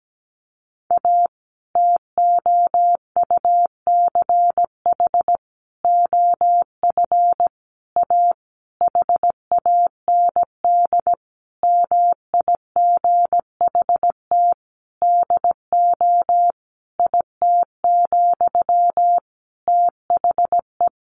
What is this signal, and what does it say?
Morse code, heard as a keyed tone, about 17 wpm, 700 Hz. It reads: A TOUCH OF A HAND MIGHT DO IT, THE
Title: A Sound Of Thunder 17wpm